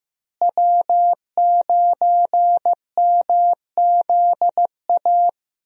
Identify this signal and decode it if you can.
Morse code, heard as a keyed tone, about 15 wpm, 700 Hz. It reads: W9MZA
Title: USA Call Signs - 5 Characters - Repeated 2x - 15wpm